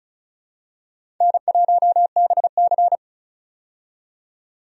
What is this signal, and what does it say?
Morse code, heard as a keyed tone, about 36 wpm, 700 Hz. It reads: N1BC
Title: USA Call Signs - 4 Characters - 35wpm